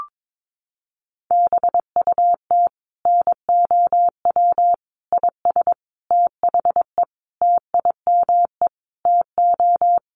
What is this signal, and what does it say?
Morse code, heard as a keyed tone, about 22 wpm, 700 Hz. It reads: BUT NOW IS THE TIME TO
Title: Sentences from Top 1500 Words 22wpm